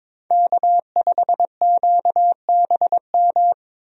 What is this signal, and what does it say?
Morse code, heard as a keyed tone, about 22 wpm, 700 Hz. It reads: K5QBM
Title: USA Call Signs - 5 Characters - 22wpm